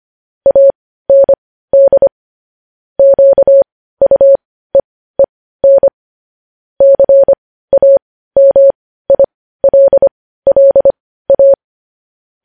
Morse code, heard as a keyed tone, about 25 wpm, spaced as for 15 wpm, 550 Hz. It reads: AND QUEEN CAMILLA
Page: News Headlines in Morse Code at 15 WPM